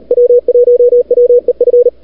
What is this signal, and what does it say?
Morse code, heard as a keyed tone, about 37 wpm, 440 Hz, with radio noise. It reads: W1WEF
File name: CWSprint / February 2007 CW Sprint / 01-2007 CW Sprint Hour1